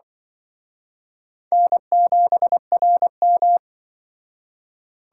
Morse code, heard as a keyed tone, about 25 wpm, 700 Hz. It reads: N7RM